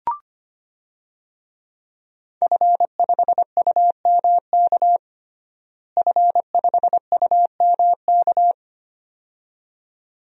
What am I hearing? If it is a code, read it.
Morse code, heard as a keyed tone, about 25 wpm, 700 Hz. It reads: F5UMK F5UMK
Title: DXCC-100 Call Signs - 5 Characters - Repeated 2x - 25wpm